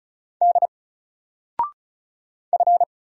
Morse code, heard as a keyed tone, about 36 wpm, 700 Hz. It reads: D F
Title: Single Letter-Number Alt 35wpm